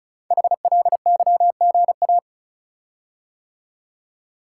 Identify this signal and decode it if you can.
Morse code, heard as a keyed tone, about 36 wpm, 700 Hz. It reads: HLYGA
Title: Code Groups - 5 Letters - 35wpm